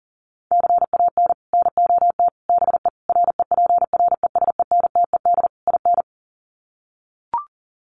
Morse code, heard as a keyed tone, about 40 wpm, 700 Hz. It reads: CAN NOT BE REPRESENTED IN
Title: Sets of 5 Words 40wpm